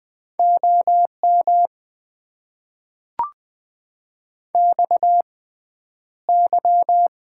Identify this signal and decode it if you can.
Morse code, heard as a keyed tone, about 20 wpm, spaced as for 10 wpm, 700 Hz. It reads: OM XY
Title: Top 90 QSO Elements - 6x Character Spacing - 20wpm